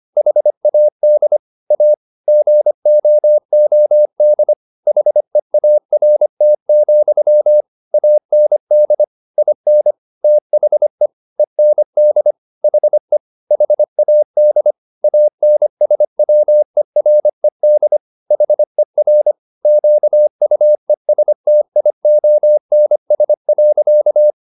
Morse code, heard as a keyed tone, about 25 wpm, 600 Hz. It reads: HAD A GOOD HEART, AND IN THE END HE HAD ANSWERED HER QUESTIONS.